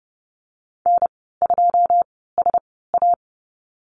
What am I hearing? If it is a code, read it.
Morse code, heard as a keyed tone, about 30 wpm, spaced as for 17 wpm, 700 Hz. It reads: N2SA
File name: Lesson 14g - Number 2 - Call Signs 30wpm